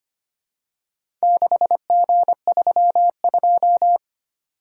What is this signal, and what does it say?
Morse code, heard as a keyed tone, about 25 wpm, 700 Hz. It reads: 6G32
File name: practice / 25wpm / Four Letter-Number Combinations - 25wpm